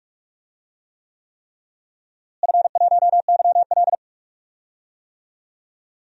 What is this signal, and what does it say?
Morse code, heard as a keyed tone, about 46 wpm, 690 Hz. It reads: F1YL
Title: DXCC-100 Call Signs - 4 Characters - Repeated 3x - 45wpm